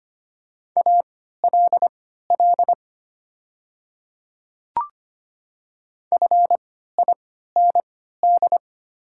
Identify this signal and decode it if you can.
Morse code, heard as a keyed tone, about 25 wpm, spaced as for 14 wpm, 700 Hz. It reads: ALL FIND
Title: 3x Character Spacing - Top 100 Words - 25wpm